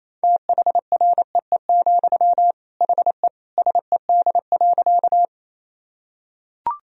Morse code, heard as a keyed tone, about 28 wpm, 700 Hz. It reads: THREE, HE SED.
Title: Sentences from Top 200 Words and Abbreviations 28wpm